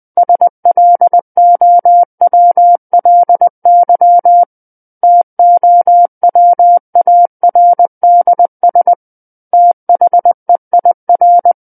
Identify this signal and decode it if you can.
Morse code, heard as a keyed tone, about 20 wpm, 700 Hz. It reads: SLOWLY TOWARDS THEIR
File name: 20 WPM